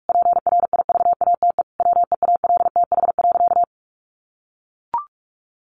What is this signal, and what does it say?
Morse code, heard as a keyed tone, about 44 wpm, 700 Hz. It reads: PRIVATE WEALTH.
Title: Sentences from Top 1200 Words 45wpm